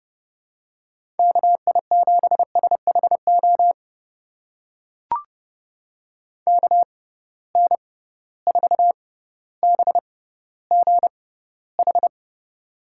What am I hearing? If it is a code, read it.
Morse code, heard as a keyed tone, about 31 wpm, spaced as for 16 wpm, 700 Hz. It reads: KI7SHO KN4BGH
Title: USA Call Signs - 6 Characters - 6x Character Spacing - 30wpm